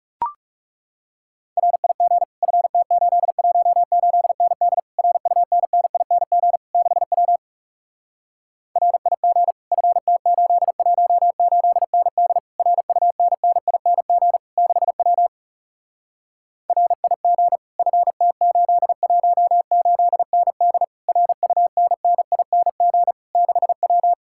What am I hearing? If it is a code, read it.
Morse code, heard as a keyed tone, about 40 wpm, 700 Hz. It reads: RIG FT818ND RUNNING 6W RIG FT818ND RUNNING 6W RIG FT818ND RUNNING 6W